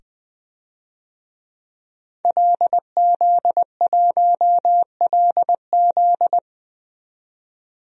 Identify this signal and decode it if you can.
Morse code, heard as a keyed tone, about 20 wpm, 700 Hz. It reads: LZ1LZ
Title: DXCC-100 Call Signs - 5 Characters - 20wpm